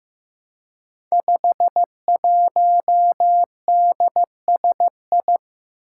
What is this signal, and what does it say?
Morse code, heard as a keyed tone, about 15 wpm, 700 Hz. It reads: 51DSI
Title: Five Challenging Letter-Number Combinations 15wpm